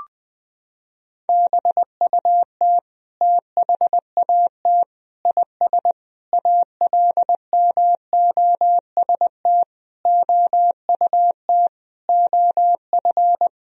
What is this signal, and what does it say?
Morse code, heard as a keyed tone, about 21 wpm, 700 Hz. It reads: BUT THAT IS ALMOST OUT OF
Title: Sentences from Top 1000 Words - Review Part 4 - 20wpm